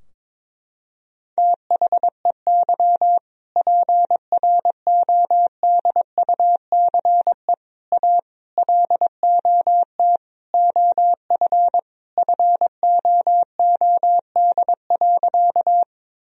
Morse code, heard as a keyed tone, about 22 wpm, 700 Hz. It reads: THEY PRODUCE A LOT OF FOOD.
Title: Sentences from Top 1000 Words - Review Part 3 - 22wpm